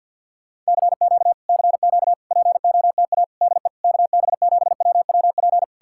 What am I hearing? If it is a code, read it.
Morse code, heard as a keyed tone, about 67 wpm, spaced as for 58 wpm, 700 Hz. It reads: CQ CQ POTA DE KB7WWP
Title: Speed-Racing Common QSO Phrases Encore - 50wpm